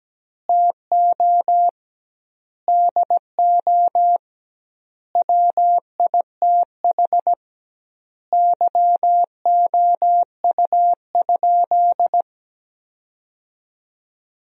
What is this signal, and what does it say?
Morse code, heard as a keyed tone, about 17 wpm, 700 Hz. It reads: TO DO WITH YOU?
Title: 2x Word-Spacing - Sentences from Top 100 Words - 17wpm